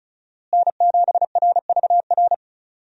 Morse code, heard as a keyed tone, about 36 wpm, 700 Hz. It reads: N7RVR
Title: USA Call Signs - 5 Characters - Repeated 2x - 35wpm